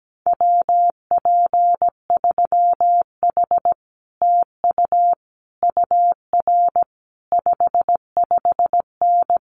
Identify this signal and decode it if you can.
Morse code, heard as a keyed tone, about 17 wpm, 700 Hz. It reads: WP3H TU UR 55N
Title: Common QSO Phrases Encore 17wpm